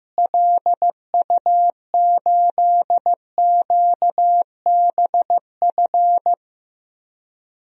Morse code, heard as a keyed tone, about 15 wpm, 700 Hz. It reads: LU8QBF
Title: DXCC-100 Call Signs - 6 Characters - Repeated 2x - 15wpm